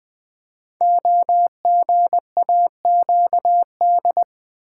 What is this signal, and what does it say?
Morse code, heard as a keyed tone, about 20 wpm, 700 Hz. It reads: OGAQD
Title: Code Groups - 5 Letters - 20wpm